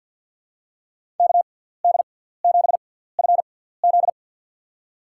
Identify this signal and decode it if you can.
Morse code, heard as a keyed tone, about 50 wpm, spaced as for 17 wpm, 700 Hz. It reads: KD7FZ
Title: USA Call Signs - 5 Characters - 6x Character Spacing - 50wpm